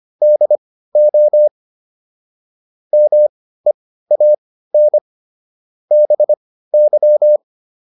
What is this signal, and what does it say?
Morse code, heard as a keyed tone, about 25 wpm, spaced as for 15 wpm, 600 Hz. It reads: DO MEAN BY
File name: WD404 15